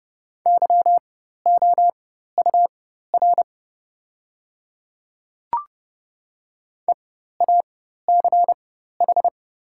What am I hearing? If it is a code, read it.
Morse code, heard as a keyed tone, about 30 wpm, spaced as for 14 wpm, 700 Hz. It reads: YOUR EACH
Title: 4x Character Spacing - Top 100 Words - 30wpm